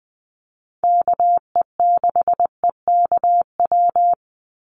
Morse code, heard as a keyed tone, about 20 wpm, 700 Hz. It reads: KE6EKW
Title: USA Call Signs 20wpm